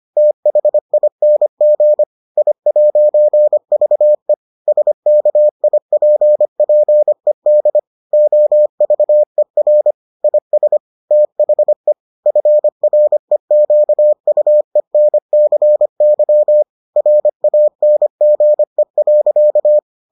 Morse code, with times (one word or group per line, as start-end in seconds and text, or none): 0.2-2.0 THING
2.4-4.3 I'VE
4.7-7.8 SKIPPED
8.1-9.9 OVER
10.2-10.8 IS
11.1-11.9 THE
12.3-16.6 FREQUENCY
17.0-19.8 RANGE.